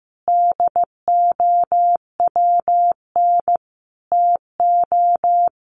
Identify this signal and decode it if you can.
Morse code, heard as a keyed tone, about 15 wpm, 700 Hz. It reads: DOWN TO
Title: Sentences from Top 1500 Words 15wpm